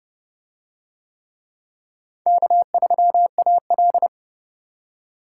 Morse code, heard as a keyed tone, about 28 wpm, 700 Hz. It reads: K3AL